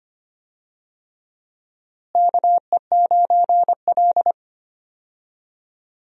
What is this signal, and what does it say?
Morse code, heard as a keyed tone, about 25 wpm, 700 Hz. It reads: KE9L
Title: USA Call Signs - 4 Characters - Repeated 3x - 25wpm